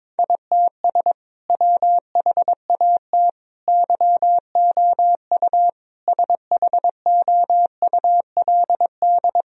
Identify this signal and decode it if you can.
Morse code, heard as a keyed tone, about 22 wpm, 700 Hz. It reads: ITS WHAT YOU SHOULD
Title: Sentences from Top 1000 Words - Review Part 3 - 22wpm